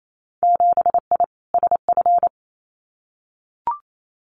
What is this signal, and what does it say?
Morse code, heard as a keyed tone, about 28 wpm, 700 Hz. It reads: 7I SF
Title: Winter Field Day Exchanges Encore 28wpm